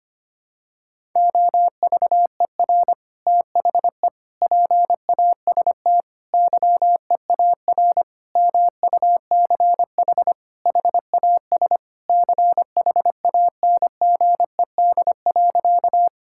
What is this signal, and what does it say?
Morse code, heard as a keyed tone, about 25 wpm, 700 Hz. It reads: OVER THE PAST YEAR MUCH HAS CHANGED.
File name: practice / 25wpm / Sentences with 3-Word Phrases - 25wpm